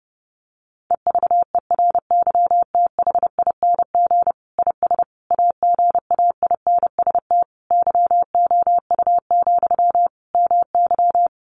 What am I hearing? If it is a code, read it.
Morse code, heard as a keyed tone, about 30 wpm, 700 Hz. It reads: EVERYTHING IS AGAINST YOU, MY
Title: Sentences from Top 700 Words 30wpm